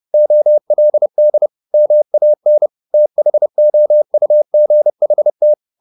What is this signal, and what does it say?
Morse code, heard as a keyed tone, about 30 wpm, 600 Hz. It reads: OLD MAN THOUGHT
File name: the old man and the sea / the old man and the sea 1 30wpm